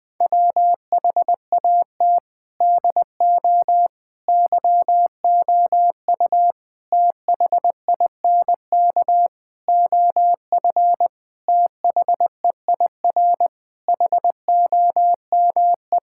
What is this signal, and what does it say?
Morse code, heard as a keyed tone, about 20 wpm, 700 Hz. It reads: WHAT DO YOU THINK OF THEIR HOME
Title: Sentences from Top 1000 Words - Review Part 5 - 20wpm